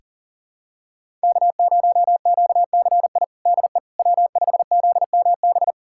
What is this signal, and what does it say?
Morse code, heard as a keyed tone, about 40 wpm, 700 Hz. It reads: K0QCI DE W5ZMB
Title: USA Callsign DE Callsign - 5 Characters - 40wpm